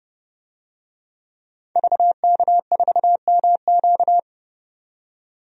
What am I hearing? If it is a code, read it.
Morse code, heard as a keyed tone, about 30 wpm, 700 Hz. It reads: VK4MQ
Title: DXCC-100 Call Signs - 5 Characters - 30wpm